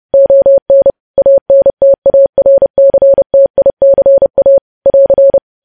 Morse code, heard as a keyed tone, about 30 wpm, 550 Hz. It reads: ON ANTARCTICA <AR>
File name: podcast / News headlines-1x-30x30